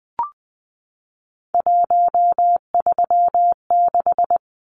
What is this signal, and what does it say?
Morse code, heard as a keyed tone, about 20 wpm, 700 Hz. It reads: 136
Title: Numbers - 3-Digit 20wpm